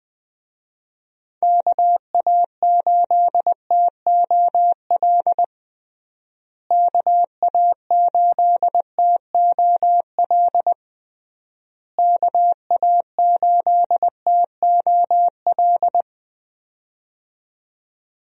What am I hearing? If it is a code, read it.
Morse code, heard as a keyed tone, about 21 wpm, 700 Hz. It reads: KA8TOL KA8TOL KA8TOL
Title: USA Call Signs - 6 Characters - Repeated 3x - 20wpm